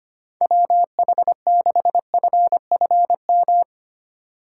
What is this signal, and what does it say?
Morse code, heard as a keyed tone, about 25 wpm, 700 Hz. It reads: WH6FFM
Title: USA Call Signs - 6 Characters - 2x Character Spacing - 25wpm